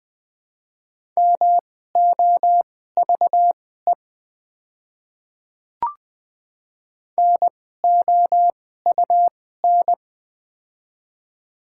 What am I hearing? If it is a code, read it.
Morse code, heard as a keyed tone, about 20 wpm, spaced as for 14 wpm, 700 Hz. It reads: MOVE NOUN
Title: 2x Character Spacing - Top 500 Words - 20wpm